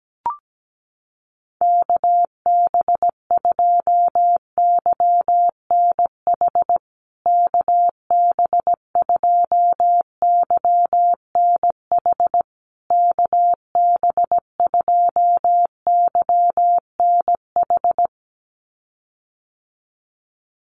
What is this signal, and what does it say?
Morse code, heard as a keyed tone, about 17 wpm, 700 Hz. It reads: KB2YNH KB2YNH KB2YNH
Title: USA Call Signs Repeated 3x 17wpm